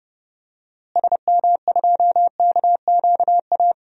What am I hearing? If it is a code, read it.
Morse code, heard as a keyed tone, about 30 wpm, 700 Hz. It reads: SM2KQA
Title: DXCC-100 Call Signs - 6 Characters - Repeated 3x - 30wpm